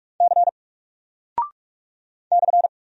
Morse code, heard as a keyed tone, about 46 wpm, 700 Hz. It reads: C /
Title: Single Character (Letter:Number:Punctuation:Prosign) 45wpm